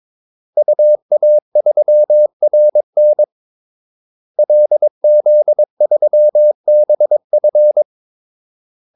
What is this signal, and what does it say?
Morse code, heard as a keyed tone, about 22 wpm, 600 Hz. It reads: UA3RN LZ3BF
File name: Callsigns-22wpm